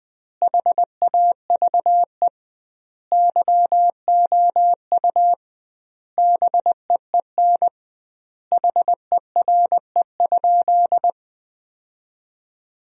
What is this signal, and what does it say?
Morse code, heard as a keyed tone, about 20 wpm, 700 Hz. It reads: HAVE YOU BEEN HERE?
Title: 2x Word-Spacing - Sentences from Top 200 Words - 20wpm